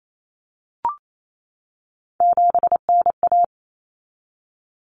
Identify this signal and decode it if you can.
Morse code, heard as a keyed tone, about 28 wpm, 700 Hz. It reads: 7NA
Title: Common Cut Numbers - 3-Digit 28wpm